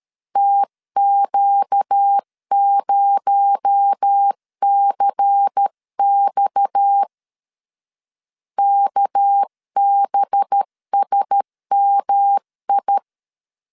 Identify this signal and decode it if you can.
Morse code, heard as a keyed tone, about 13 wpm, 800 Hz. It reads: TQ0CX KBSMI
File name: CW / DET / 1 6 1500C S C 12wpm 3533